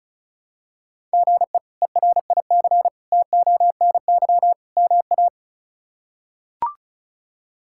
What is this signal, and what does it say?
Morse code, heard as a keyed tone, about 36 wpm, 700 Hz. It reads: GE ERIC TONY MA
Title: K1USN SST QSO Practice - 35wpm